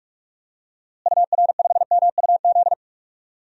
Morse code, heard as a keyed tone, about 46 wpm, 700 Hz. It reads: UR5MUZ
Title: DXCC-100 Call Signs - 6 Characters - Repeated 2x - 45wpm